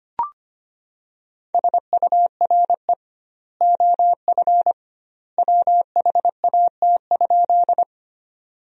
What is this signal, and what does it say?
Morse code, heard as a keyed tone, about 25 wpm, 700 Hz. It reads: SURE OF WHAT?
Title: 2x Word-Spacing - Sentences from Top 300 Words - 25wpm